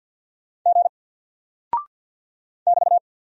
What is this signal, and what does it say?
Morse code, heard as a keyed tone, about 48 wpm, 700 Hz. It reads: G <BT>
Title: Single Character (Letter:Number:Punctuation:Prosign) Alt 50wpm